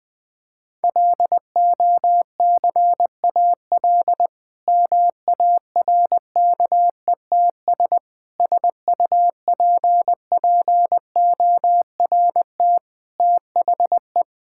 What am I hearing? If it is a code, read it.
Morse code, heard as a keyed tone, about 20 wpm, 700 Hz. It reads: LOCAL MARKETS SUPPORT THE
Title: Difficult Sentences with Hints - 20wpm